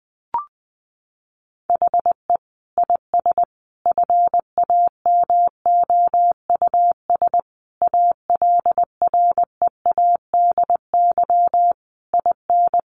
Morse code, heard as a keyed tone, about 19 wpm, 700 Hz. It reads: HE IS FAMOUS ALREADY IN
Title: Sentences from Top 1500 Words 20wpm